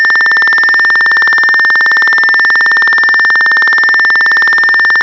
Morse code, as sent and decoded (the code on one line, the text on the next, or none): none